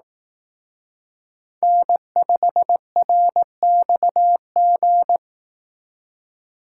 Morse code, heard as a keyed tone, about 18 wpm, 700 Hz. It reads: N5RXG
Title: Speed Racing - USA 1x3 Call Signs 15wpm